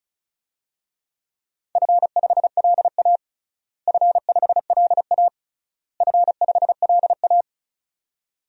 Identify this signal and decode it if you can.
Morse code, heard as a keyed tone, about 34 wpm, 700 Hz. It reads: F5LA F5LA F5LA
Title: DXCC-100 Call Signs - 4 Characters - Repeated 3x - 35wpm